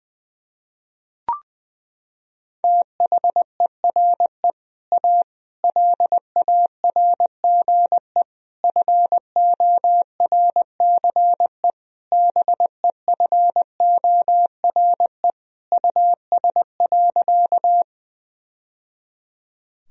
Morse code, heard as a keyed tone, about 20 wpm, 700 Hz. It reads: THERE A LARGE FORCE BEFORE US.
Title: Sentences from Top 1000 Words - Review Part 3 - 20wpm